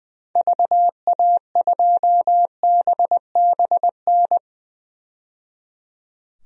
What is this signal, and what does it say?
Morse code, heard as a keyed tone, about 20 wpm, 700 Hz. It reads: VA2BBN
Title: DXCC-100 Call Signs - 6 Characters - 20wpm